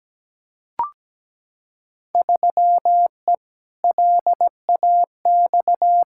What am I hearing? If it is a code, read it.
Morse code, heard as a keyed tone, about 17 wpm, 700 Hz. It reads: 3E LAX
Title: Field Day Exchanges Encore - 17wpm